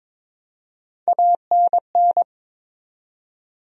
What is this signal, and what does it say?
Morse code, heard as a keyed tone, about 22 wpm, 700 Hz. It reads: ANN
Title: USA Common Names - Max 3 Letters - 22wpm